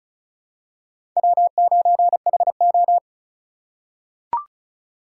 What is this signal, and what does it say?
Morse code, heard as a keyed tone, about 34 wpm, 700 Hz. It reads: W9HO
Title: USA Call Signs - 4 Characters - 6x Character Spacing - 35wpm